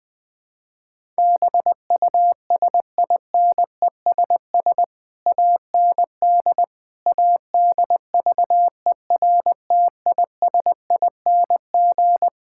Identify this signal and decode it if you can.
Morse code, heard as a keyed tone, about 20 wpm, 700 Hz. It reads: BUSINESS AND ADVERTISING
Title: Sentences from Top 1000 Words - Review Part 6 - 20wpm